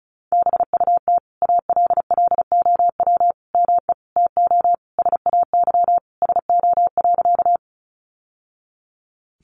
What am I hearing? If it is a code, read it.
Morse code, heard as a keyed tone, about 34 wpm, 700 Hz. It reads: BUT ALLOW ME TO SAY SO.
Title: Sentences from Top 700 Words 35wpm